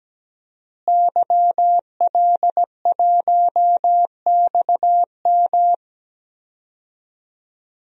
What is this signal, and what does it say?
Morse code, heard as a keyed tone, about 17 wpm, 700 Hz. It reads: YL1XM
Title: DXCC-100 Call Signs - 5 Characters - Repeated 2x - 17wpm